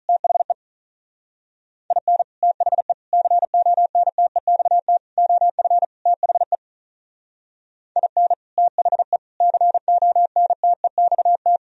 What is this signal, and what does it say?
Morse code, heard as a keyed tone, about 42 wpm, 700 Hz. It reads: THE IN THE CONTEXT OF THE IN THE CONTEXT
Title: Speed-Racing - Sets of 5 Words 35wpm